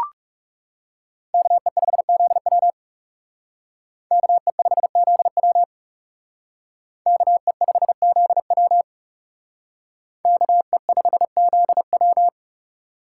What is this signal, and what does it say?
Morse code, heard as a keyed tone, about 36 wpm, 700 Hz. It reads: KE5ZW KE5ZW KE5ZW KE5ZW
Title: Speed Racing - USA 2x2 Call Signs 30wpm